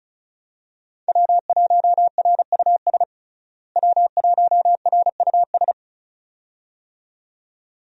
Morse code, heard as a keyed tone, about 36 wpm, 700 Hz. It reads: W1RUS W1RUS
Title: USA Call Signs - 5 Characters - Repeated 2x - 35wpm